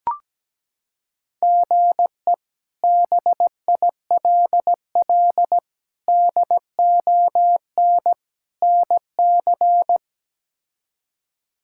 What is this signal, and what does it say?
Morse code, heard as a keyed tone, about 17 wpm, 700 Hz. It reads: GE BILL DON NC
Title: K1USN SST QSO Practice - 17wpm